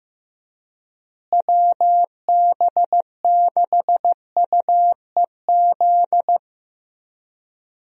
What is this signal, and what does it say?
Morse code, heard as a keyed tone, about 15 wpm, 700 Hz. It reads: WB6UEZ
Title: USA Call Signs - 6 Characters - Repeated 2x - 15wpm